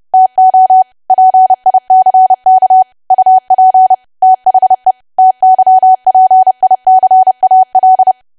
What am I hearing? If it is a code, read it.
Morse code, heard as a keyed tone, about 30 wpm, 750 Hz. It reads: TO PICK UP THE TYPICAL
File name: cwa / W1AW-Level3 / 30WPM